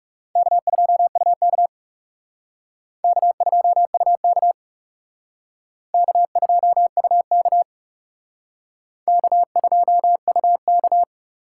Morse code, heard as a keyed tone, about 37 wpm, 700 Hz. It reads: K2UK K2UK K2UK K2UK
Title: Speed Racing - USA 1x2 Call Signs 30wpm